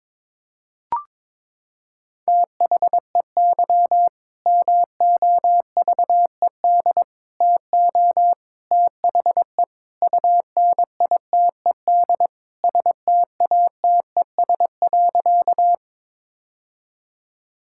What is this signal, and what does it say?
Morse code, heard as a keyed tone, about 22 wpm, 700 Hz. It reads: THEY MOVED TO THE UNITED STATES.
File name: practice / 22wpm / Sentences with 4-Word Phrases - 22wpm